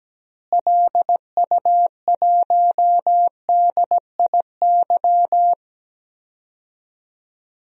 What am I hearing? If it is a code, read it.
Morse code, heard as a keyed tone, about 17 wpm, 700 Hz. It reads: LU1DIY
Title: DXCC-100 Call Signs - 6 Characters - Repeated 3x - 17wpm